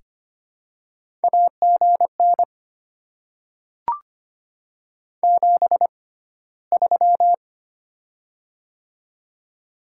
Morse code, heard as a keyed tone, about 25 wpm, spaced as for 15 wpm, 700 Hz. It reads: AGN 73
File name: Top 20 QSO Elements - 6x Character Spacing - 25wpm